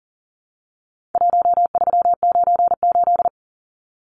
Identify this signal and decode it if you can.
Morse code, heard as a keyed tone, about 40 wpm, 700 Hz. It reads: 1398
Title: Numbers - Contest Style - 4-Digit 40wpm